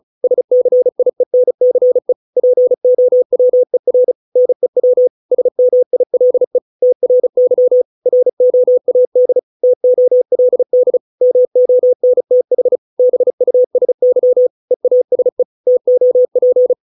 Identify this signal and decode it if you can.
Morse code, heard as a keyed tone, about 34 wpm, 500 Hz. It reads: SCIENCE POWER NEW SMILE TRY ROAD TOLD MONTH BUSY EASE TOP